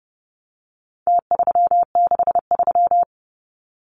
Morse code, heard as a keyed tone, about 30 wpm, 700 Hz. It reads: T363
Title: Numbers - Contest Style - 4-Digit 30wpm